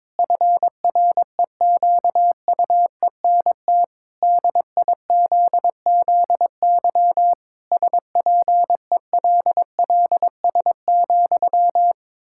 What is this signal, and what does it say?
Morse code, heard as a keyed tone, about 22 wpm, 700 Hz. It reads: FREQUENT DIZZY SPELLS,
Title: Pangram Challenge v2 - 22wpm